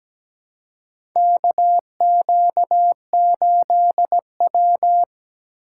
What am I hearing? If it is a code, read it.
Morse code, heard as a keyed tone, about 17 wpm, 700 Hz. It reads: KQ8W
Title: USA Call Signs - 4 Characters - 2x Character Spacing - 17wpm